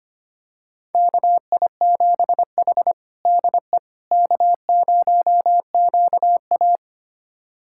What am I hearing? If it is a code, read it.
Morse code, heard as a keyed tone, about 25 wpm, 700 Hz. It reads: KI7H DE K0QA
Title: USA Callsign DE Callsign - 4 Characters - 25wpm